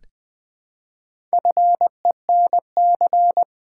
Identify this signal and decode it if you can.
Morse code, heard as a keyed tone, about 20 wpm, 700 Hz. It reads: FENC
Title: Build A Word - 1 Letter At A Time - 5-Letter Words - 20wpm